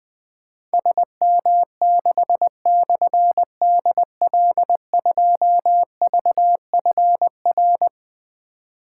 Morse code, heard as a keyed tone, about 21 wpm, 700 Hz. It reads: SM6/DL2VFR
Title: Difficult Call Signs 20wpm